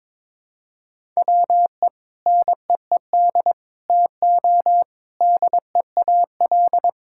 Morse code, heard as a keyed tone, about 22 wpm, 700 Hz. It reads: WE NEED TO DEAL
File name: practice / 22wpm / Sentences with 4-Word Phrases - 22wpm